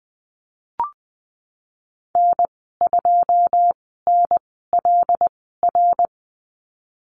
Morse code, heard as a keyed tone, about 20 wpm, spaced as for 15 wpm, 700 Hz. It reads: N2NLR
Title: 088 Lesson 15g - Number 5 - Call Signs 20wpm